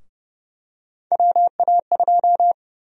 Morse code, heard as a keyed tone, about 30 wpm, 700 Hz. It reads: WA2
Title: USA Call Sign Prefixes - Repeated 2x - 30wpm